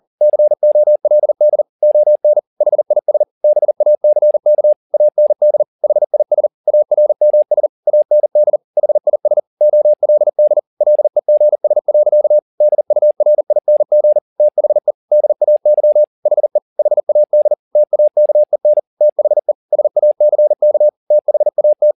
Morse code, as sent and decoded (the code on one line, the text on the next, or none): -.-. --- .-.. -.. / --- -. / .... .. ... / -... .- -.-. -.- / .- -. -.. / .... .. ... / .- .-. -- ... / .- -. -.. / .... .. ... / --- .-.. -.. / .-.. --. ... .-.-.- / -.. ..- .-. .. -. --. / - .... / -.. .- -.-- / .... / .... .- -.. / - .- -.- . -. / - .... / ... .- -.-. -.- / - .... .- -
COLD ON HIS BACK AND HIS ARMS AND HIS OLD LEGS. DURING THE DAY HE HAD TAKEN THE SACK THAT